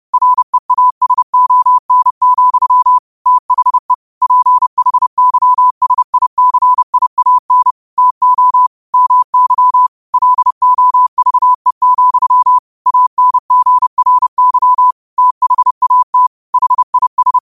Morse code, heard as a keyed tone, about 30 wpm, 1000 Hz. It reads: REASON, THE PHYSICIAN TO MY LOVE, ANGRY THAT HIS
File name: qotdfiles / 30wpm